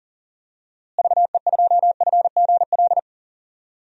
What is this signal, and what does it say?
Morse code, heard as a keyed tone, about 40 wpm, 700 Hz. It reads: VE2FGL
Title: DXCC-100 Call Signs - 6 Characters - Repeated 3x - 40wpm